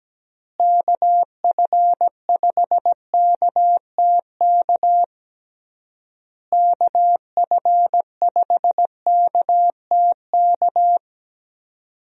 Morse code, heard as a keyed tone, about 17 wpm, 700 Hz. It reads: KF5KTK KF5KTK
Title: USA Call Signs - 6 Characters - Repeated 3x - 17wpm